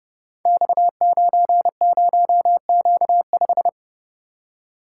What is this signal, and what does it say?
Morse code, heard as a keyed tone, about 30 wpm, 700 Hz. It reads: X90Q5
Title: Code Groups - 5 Letter:Numbers - 30wpm